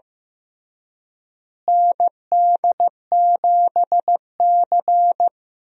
Morse code, heard as a keyed tone, about 15 wpm, 700 Hz. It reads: ND7C
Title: USA Call Signs - 4 Characters - Repeated 3x - 15wpm